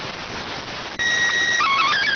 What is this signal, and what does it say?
SSTV (slow-scan television), PD 290.